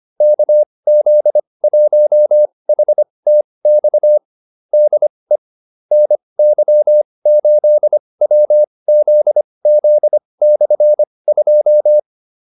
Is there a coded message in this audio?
Yes: Morse code from 0.2 to 12.0 s, heard as a keyed tone, about 25 wpm, spaced as for 20 wpm, 600 Hz. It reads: KZ1HTX DE NY8WZZ/2